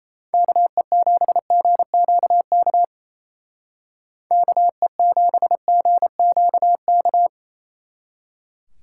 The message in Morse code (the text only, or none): KE7GQK KE7GQK